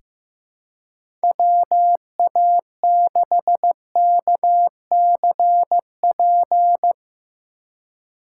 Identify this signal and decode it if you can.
Morse code, heard as a keyed tone, about 16 wpm, 700 Hz. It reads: WA6KCP
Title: USA Call Signs - 6 Characters - 6x Character Spacing - 15wpm